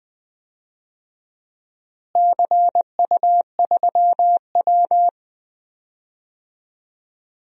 Morse code, heard as a keyed tone, about 20 wpm, 700 Hz. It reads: CU3W
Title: DXCC-100 Call Signs - 4 Characters - Repeated 3x - 20wpm